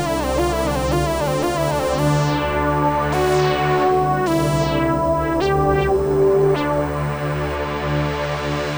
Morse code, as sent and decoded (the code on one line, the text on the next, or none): none